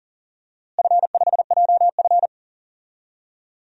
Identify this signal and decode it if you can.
Morse code, heard as a keyed tone, about 39 wpm, 700 Hz. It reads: F5JF
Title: DXCC-100 Call Signs - 4 Characters - Repeated 3x - 40wpm